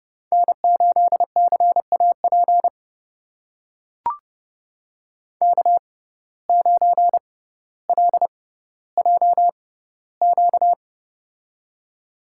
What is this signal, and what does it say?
Morse code, heard as a keyed tone, about 30 wpm, spaced as for 15 wpm, 700 Hz. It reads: N8CAP K9LJQ
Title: USA Call Signs - 5 Characters - 6x Character Spacing - 30wpm